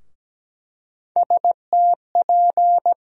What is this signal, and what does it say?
Morse code, heard as a keyed tone, about 17 wpm, 700 Hz. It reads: STP
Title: USA Call Sign Postfixes - 3 Letters 17wpm